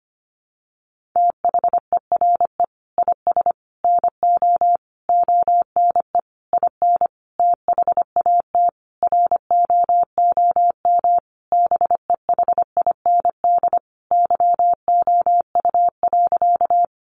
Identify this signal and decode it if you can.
Morse code, heard as a keyed tone, about 25 wpm, 700 Hz. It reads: THERE IS NO ONE IN THAT ROOM BEHIND YOU.
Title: Sentences from Top 500 Words 25wpm